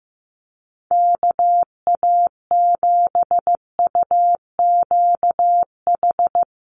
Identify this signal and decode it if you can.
Morse code, heard as a keyed tone, about 15 wpm, 700 Hz. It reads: KA7UQH
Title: USA Call Signs Repeated 2x 15wpm